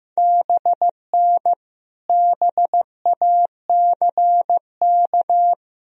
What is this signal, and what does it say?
Morse code, heard as a keyed tone, about 16 wpm, 700 Hz. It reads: BN BACK
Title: Sentences from Top 200 Words and Abbreviations 15wpm